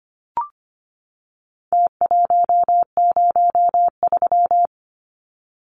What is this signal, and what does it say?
Morse code, heard as a keyed tone, about 25 wpm, 700 Hz. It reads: T103
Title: Numbers - Contest Style - 4-Digit 25wpm